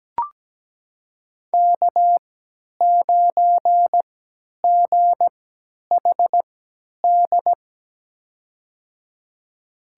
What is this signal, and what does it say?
Morse code, heard as a keyed tone, about 17 wpm, spaced as for 10 wpm, 700 Hz. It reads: K9GHD
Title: USA Call Signs - 3x Character Spacing - 17wpm